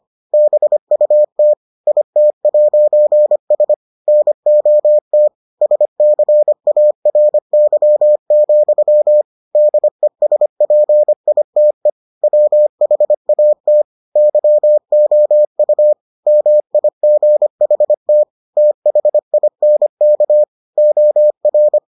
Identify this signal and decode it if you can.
Morse code, heard as a keyed tone, about 25 wpm, 600 Hz. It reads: BUT IT'S NOT SCARY, DESPITE WHAT YOU MIGHT THINK OR